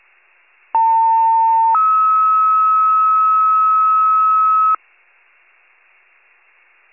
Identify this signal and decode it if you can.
Two-tone page, heard as a one-second tone, then a three-second tone, about 900 then 1290 Hz. It is heard from 0.7 to 4.8 s.
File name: wcalert